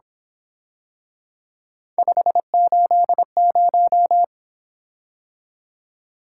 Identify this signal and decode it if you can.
Morse code, heard as a keyed tone, about 26 wpm, 700 Hz. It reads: 580
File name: Speed-Racing - Three Letter-Number Combinations - 22wpm